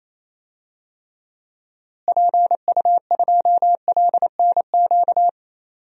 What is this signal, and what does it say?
Morse code, heard as a keyed tone, about 28 wpm, 700 Hz. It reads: PU2LNQ
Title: DXCC-100 Call Signs - 6 Characters - Repeated 2x - 28wpm